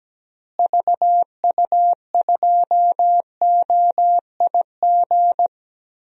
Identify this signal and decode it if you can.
Morse code, heard as a keyed tone, about 17 wpm, 700 Hz. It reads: VU2OIG
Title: DXCC-100 Call Signs - 6 Characters - Repeated 2x - 17wpm